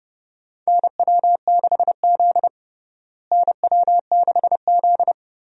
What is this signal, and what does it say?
Morse code, heard as a keyed tone, about 30 wpm, 700 Hz. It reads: NW6Z NW6Z
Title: USA Call Signs - 4 Characters - Repeated 3x - 30wpm